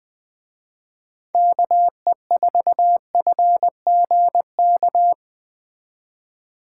Morse code, heard as a keyed tone, about 20 wpm, 700 Hz. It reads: KE4FGK
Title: Speed Racing - USA 2x3 Call Signs 17wpm